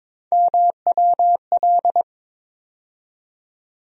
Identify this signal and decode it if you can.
Morse code, heard as a keyed tone, about 21 wpm, 700 Hz. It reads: MWL
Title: USA Call Sign Postfixes - 3 Letters 22wpm